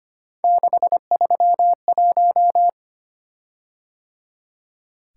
Morse code, heard as a keyed tone, about 25 wpm, 700 Hz. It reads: 631
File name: Three Letter-Number Combinations - 25wpm